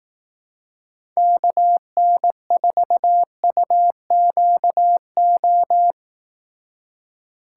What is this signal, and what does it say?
Morse code, heard as a keyed tone, about 17 wpm, 700 Hz. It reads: KN4UQO
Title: Speed Racing - USA 2x3 Call Signs 15wpm